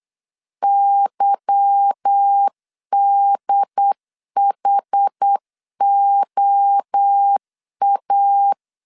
Morse code, heard as a keyed tone, about 8 wpm, 800 Hz. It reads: YDHOA
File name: LEZIONI / 8 YJ45 / 8 Y 8wpm 3245